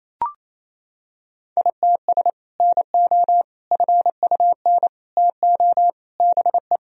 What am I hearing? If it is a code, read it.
Morse code, heard as a keyed tone, about 28 wpm, 700 Hz. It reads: ITS NO FUN TO BE
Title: Sentences from Top 1000 Words - Review Part 4 - 28wpm